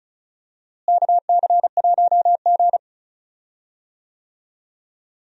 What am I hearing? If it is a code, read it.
Morse code, heard as a keyed tone, about 36 wpm, 700 Hz. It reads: KC1G
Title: USA Call Signs - 4 Characters - Repeated 3x - 35wpm